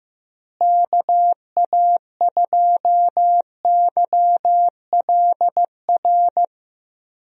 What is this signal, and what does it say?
Morse code, heard as a keyed tone, about 15 wpm, 700 Hz. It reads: KA2YLR
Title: USA Call Signs - 6 Characters - Repeated 3x - 15wpm